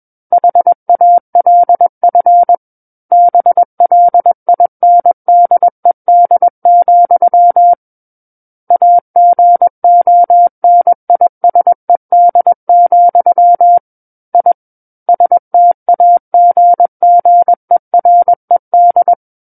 Morse code, heard as a keyed tone, about 21 wpm, 700 Hz. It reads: HALF BLINDED, AGONISED, I STAGGERED
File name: Book One / 12-Chapter 12 - 21 WPM